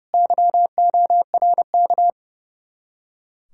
Morse code, heard as a keyed tone, about 28 wpm, 700 Hz. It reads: YORK
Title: Speed-Racing Common QSO Phrases Encore - 30wpm